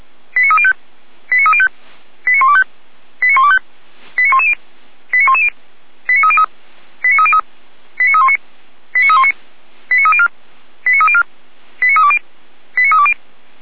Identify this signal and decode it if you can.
Five-tone selective call (ZVEI), heard as quick runs of five tones, tones of about 80 ms. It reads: 89305 89305 89125 89125 89110 89110 89303 89303 89319 89319 89304 89304 89320 89320
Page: /455/SDR/ZVEI/89265/2026.02.04/